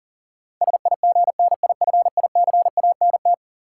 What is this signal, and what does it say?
Morse code, heard as a keyed tone, about 40 wpm, 700 Hz. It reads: SIGNIFICANT
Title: Sentences from Top 1000 Words - Review Part 6 - New Word Drill - 40wpm